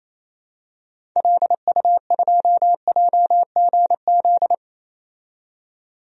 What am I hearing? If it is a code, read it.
Morse code, heard as a keyed tone, about 28 wpm, 700 Hz. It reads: LU2JGZ
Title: DXCC-100 Call Signs - 6 Characters - 28wpm